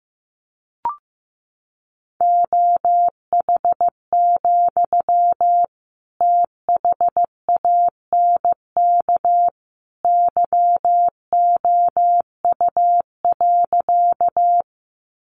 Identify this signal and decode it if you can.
Morse code, heard as a keyed tone, about 15 wpm, 700 Hz. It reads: OH, THANK YOU.
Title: Sentences from Top 600 Words 15wpm